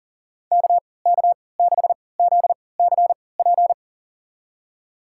Morse code, heard as a keyed tone, about 40 wpm, spaced as for 23 wpm, 700 Hz. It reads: KK6ZCP
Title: USA Call Signs - 6 Characters - 3x Character Spacing - 40wpm